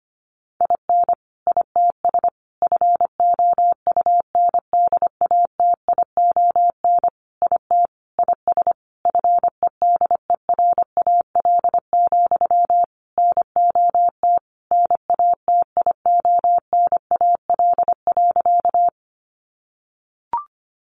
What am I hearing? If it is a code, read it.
Morse code, heard as a keyed tone, about 25 wpm, 700 Hz. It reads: IN ITS FOUNDATION IT IS FEDERAL, NOT NATIONAL.
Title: Sentences from Top 1200 Words 25wpm